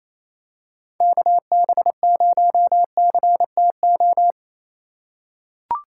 Morse code, heard as a keyed tone, about 28 wpm, 700 Hz. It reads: KB0CTO
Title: USA Call Signs - 6 Characters - 6x Character Spacing - 28wpm